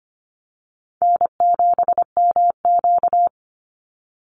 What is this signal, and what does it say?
Morse code, heard as a keyed tone, about 25 wpm, 700 Hz. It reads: N7MQ
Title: Easy Call Signs - 25wpm